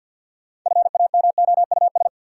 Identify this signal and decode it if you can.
Morse code, heard as a keyed tone, about 52 wpm, 700 Hz. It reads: FAMOUS
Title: Sentences from Top 1500 Words - New Word Drill - 50wpm